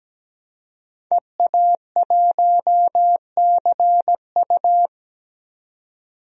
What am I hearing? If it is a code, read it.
Morse code, heard as a keyed tone, about 17 wpm, 700 Hz. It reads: EA1CU
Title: DXCC-100 Call Signs - 5 Characters - 17wpm